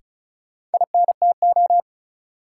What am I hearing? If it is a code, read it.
Morse code, heard as a keyed tone, about 34 wpm, 700 Hz. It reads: INTO
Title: Sets of 2 Words - New Word Drill - 35wpm